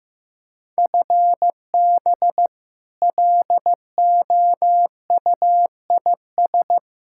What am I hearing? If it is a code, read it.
Morse code, heard as a keyed tone, about 16 wpm, 700 Hz. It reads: FB LOUIS
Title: K1USN SST QSO Practice 15wpm